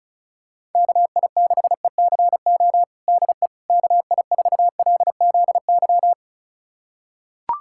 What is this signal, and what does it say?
Morse code, heard as a keyed tone, about 36 wpm, 700 Hz. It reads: KI6ECO DE KI4LZY
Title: USA Callsign DE Callsign - 6 Characters - 35wpm